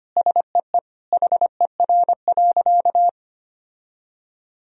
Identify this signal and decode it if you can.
Morse code, heard as a keyed tone, about 25 wpm, 700 Hz. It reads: SEE HER.
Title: Sentences from Top 1000 Words - Review Part 4 - 25wpm